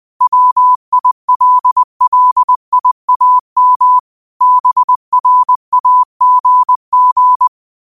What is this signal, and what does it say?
Morse code, heard as a keyed tone, about 20 wpm, 1000 Hz. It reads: WILLIAM BRAGG
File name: qotdfiles / 20wpm